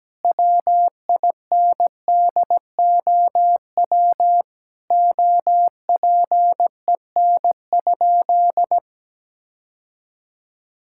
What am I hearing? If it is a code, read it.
Morse code, heard as a keyed tone, about 17 wpm, 700 Hz. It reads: WINDOW OPEN?
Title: Sentences from Top 1600 Words 17wpm